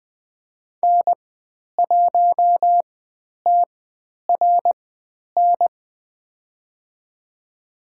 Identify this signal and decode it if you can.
Morse code, heard as a keyed tone, about 20 wpm, spaced as for 10 wpm, 700 Hz. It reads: N1TRN
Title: 095 Lesson 16g - Letter c - Call Signs 20wpm